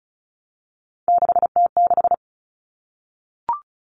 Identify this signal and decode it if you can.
Morse code, heard as a keyed tone, about 34 wpm, 700 Hz. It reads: 6T6
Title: All Cut Numbers - 3-Digit 35wpm